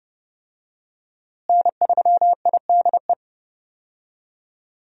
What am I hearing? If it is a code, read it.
Morse code, heard as a keyed tone, about 32 wpm, 700 Hz. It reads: N3IDE
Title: USA Call Signs - 5 Characters - Repeated 2x - 30wpm